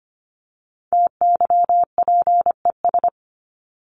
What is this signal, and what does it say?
Morse code, heard as a keyed tone, about 25 wpm, 700 Hz. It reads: TYPES
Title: ES:S Suffix - Top 100 Words with Base Words 25wpm